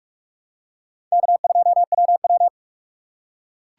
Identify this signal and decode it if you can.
Morse code, heard as a keyed tone, about 46 wpm, 690 Hz. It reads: K2WW
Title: USA Call Signs - 4 Characters - 3x Character Spacing - 45wpm